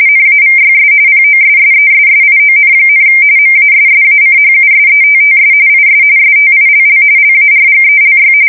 RTTY, about 45.45 baud. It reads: NAVYS POSSESSION, MASSACHUSETTS IS STILL LARGELY